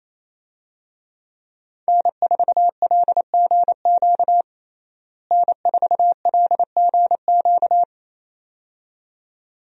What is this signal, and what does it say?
Morse code, heard as a keyed tone, about 28 wpm, 700 Hz. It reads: N4LGQ N4LGQ
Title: USA Call Signs - 5 Characters - Repeated 2x - 28wpm